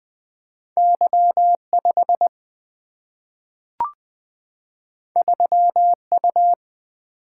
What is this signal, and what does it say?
Morse code, heard as a keyed tone, about 20 wpm, 700 Hz. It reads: Y5 3U
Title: Single-Letter Single-Number 20wpm